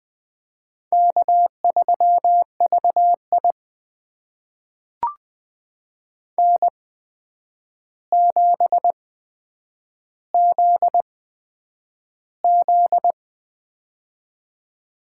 Morse code, heard as a keyed tone, about 20 wpm, spaced as for 9 wpm, 700 Hz. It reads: K3VI N7ZZ
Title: USA Call Signs - 4 Characters - 8x Character Spacing - 20wpm